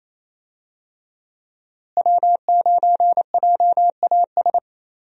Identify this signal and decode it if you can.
Morse code, heard as a keyed tone, about 27 wpm, 700 Hz. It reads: W9JAS
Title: USA Call Signs - 5 Characters - Repeated 3x - 28wpm